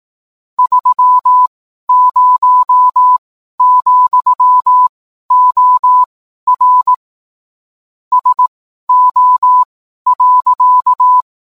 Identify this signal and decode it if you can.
Morse code, heard as a keyed tone, about 18 wpm, spaced as for 12 wpm, 1000 Hz. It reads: 30, OR SO.
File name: cwblog / cw mp3 directory / mem morse 12